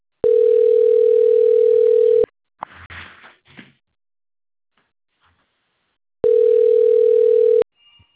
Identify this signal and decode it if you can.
Ringback tone (North America), 440 + 480 Hz.